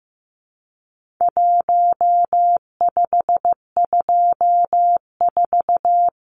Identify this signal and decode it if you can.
Morse code, heard as a keyed tone, about 15 wpm, 700 Hz. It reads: 1524
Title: Numbers - 4-Digit 15wpm